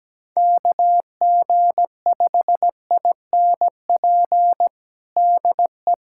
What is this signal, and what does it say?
Morse code, heard as a keyed tone, about 17 wpm, 700 Hz. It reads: KG5INP DE
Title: USA Callsign DE Callsign - 6 Characters - 17wpm